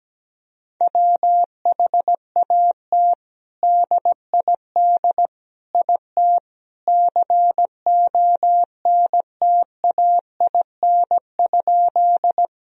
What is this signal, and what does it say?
Morse code, heard as a keyed tone, about 17 wpm, 700 Hz. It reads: WHAT DID IT CONTAIN?
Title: Sentences from Top 1000 Words - Review Part 2 - 17wpm